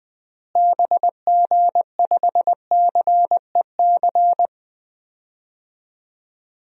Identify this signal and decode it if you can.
Morse code, heard as a keyed tone, about 20 wpm, 700 Hz. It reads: BG5CEC
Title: DXCC-100 Call Signs - 6 Characters - Repeated 3x - 20wpm